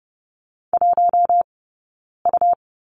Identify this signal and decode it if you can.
Morse code, heard as a keyed tone, about 30 wpm, 700 Hz. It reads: 1 U
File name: Lesson 17d - Letter u - Two Characters 30wpm